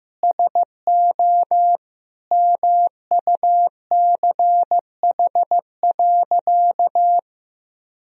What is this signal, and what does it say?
Morse code, heard as a keyed tone, about 16 wpm, 700 Hz. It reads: SO MUCH.
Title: Sentences from Top 1000 Words - Review Part 3 - 15wpm